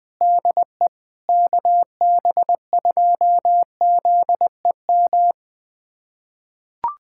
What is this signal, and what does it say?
Morse code, heard as a keyed tone, about 20 wpm, 700 Hz. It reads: DE KB2ZEM
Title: USA Callsign DE Callsign - 6 Characters - 20wpm